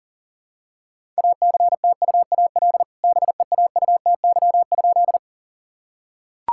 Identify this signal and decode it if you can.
Morse code, heard as a keyed tone, about 40 wpm, 700 Hz. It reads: ACTUAL BEAUTY?
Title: Sentences from Top 1000 Words - Review Part 5 - 40wpm